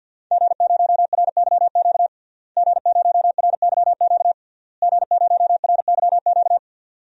Morse code, heard as a keyed tone, about 51 wpm, 700 Hz. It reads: G0RYQ G0RYQ G0RYQ
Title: DXCC-100 Call Signs - 5 Characters - Repeated 3x - 50wpm